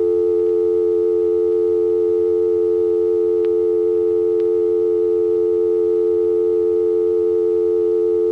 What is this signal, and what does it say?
Dial tone (North America), 350 + 440 Hz.